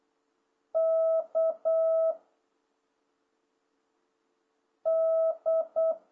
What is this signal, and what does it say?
Morse code, heard as a keyed tone, about 8 wpm, 660 Hz. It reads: K D